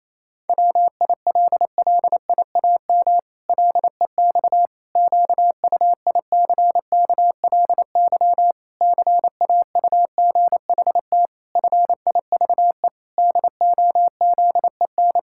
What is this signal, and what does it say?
Morse code, heard as a keyed tone, about 28 wpm, 700 Hz. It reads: WILLIAM LEX QUICKLY CAUGHT FIVE DOZEN
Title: Pangram Challenge v2 - 28wpm